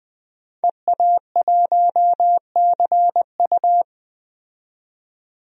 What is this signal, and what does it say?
Morse code, heard as a keyed tone, about 20 wpm, 700 Hz. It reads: EA1CU
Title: DXCC-100 Call Signs - 5 Characters - 20wpm